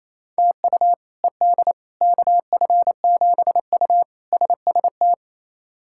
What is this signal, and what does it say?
Morse code, heard as a keyed tone, about 28 wpm, 700 Hz. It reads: TU ED KF7U SST
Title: K1USN SST QSO Practice - 28wpm